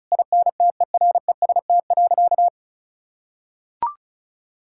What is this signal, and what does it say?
Morse code, heard as a keyed tone, about 36 wpm, 700 Hz. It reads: INTEREST.
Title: Sentences from Top 1000 Words - Review Part 5 - 35wpm